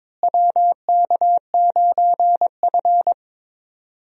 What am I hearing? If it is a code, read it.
Morse code, heard as a keyed tone, about 22 wpm, 700 Hz. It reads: WK9F
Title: USA Call Signs - 4 Characters - Repeated 3x - 22wpm